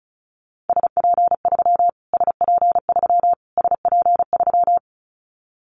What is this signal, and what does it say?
Morse code, heard as a keyed tone, about 34 wpm, 700 Hz. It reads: SP3 SP3 SP3
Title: Callsign Prefixes for Top 25 DXCC - Repeated 3x - 35wpm